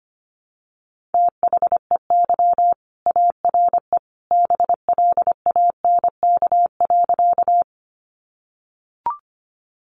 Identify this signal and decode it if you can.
Morse code, heard as a keyed tone, about 25 wpm, 700 Hz. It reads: THEY ARE BLANK.
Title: Sentences from Top 1500 Words 25wpm